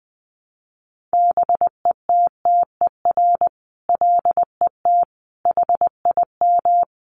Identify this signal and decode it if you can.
Morse code, heard as a keyed tone, about 20 wpm, 700 Hz. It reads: BETTER LET HIM
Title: Sentences from Top 1200 Words 20wpm